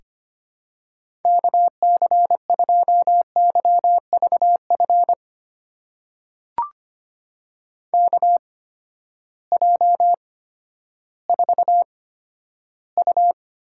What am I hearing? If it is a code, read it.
Morse code, heard as a keyed tone, about 25 wpm, spaced as for 12 wpm, 700 Hz. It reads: KC2YVF KJ4U
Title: USA Call Signs - 6 Characters - 8x Character Spacing - 25wpm